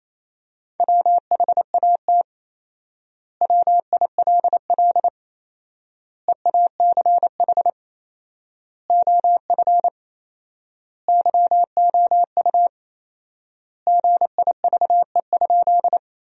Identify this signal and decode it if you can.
Morse code, heard as a keyed tone, about 28 wpm, 700 Hz. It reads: WHAT WILL EACH OF YOU GIVE?
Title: 4x Word-Spacing - Sentences from Top 200 Words - 28wpm